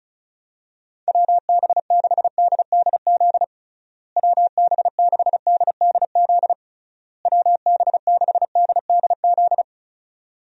Morse code, heard as a keyed tone, about 34 wpm, 700 Hz. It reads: WB6DDZ WB6DDZ WB6DDZ
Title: USA Call Signs - 6 Characters - Repeated 3x - 35wpm